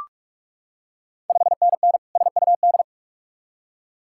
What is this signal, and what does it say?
Morse code, heard as a keyed tone, about 41 wpm, 690 Hz. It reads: 5NN SUD
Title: Ontario QSO Party Exchange 45wpm